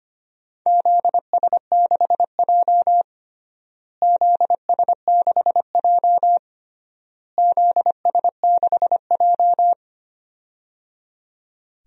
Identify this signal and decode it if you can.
Morse code, heard as a keyed tone, about 25 wpm, 700 Hz. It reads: ZS6J ZS6J ZS6J
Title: DXCC-100 Call Signs - 4 Characters - Repeated 3x - 25wpm